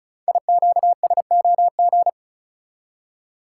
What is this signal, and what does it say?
Morse code, heard as a keyed tone, about 36 wpm, 690 Hz. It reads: IQSOG
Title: Code Groups - 5 Letters - 35wpm